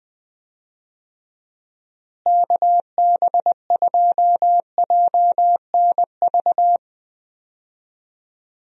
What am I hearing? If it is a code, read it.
Morse code, heard as a keyed tone, about 20 wpm, 700 Hz. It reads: KB2JNV
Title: USA Call Signs - 6 Characters - Repeated 3x - 20wpm